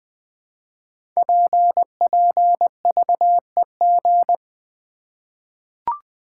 Morse code, heard as a keyed tone, about 20 wpm, 700 Hz. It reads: PPVEG
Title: Code Groups - 5 Letter:Numbers - 20wpm